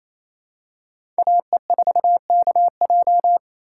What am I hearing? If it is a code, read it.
Morse code, heard as a keyed tone, about 27 wpm, 700 Hz. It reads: AE4KJ
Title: USA Call Signs - 5 Characters - 2x Character Spacing - 28wpm